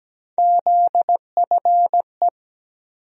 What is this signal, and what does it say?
Morse code, heard as a keyed tone, about 17 wpm, 700 Hz. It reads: ZFE
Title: USA Call Sign Postfixes - 3 Letters 17wpm